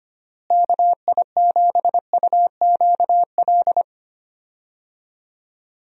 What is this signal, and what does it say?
Morse code, heard as a keyed tone, about 25 wpm, 700 Hz. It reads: KI7UQL
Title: USA Call Signs - 6 Characters - Repeated 2x - 25wpm